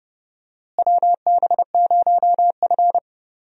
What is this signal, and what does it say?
Morse code, heard as a keyed tone, about 30 wpm, 700 Hz. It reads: WB0F
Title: USA Call Signs - 4 Characters - Repeated 2x - 30wpm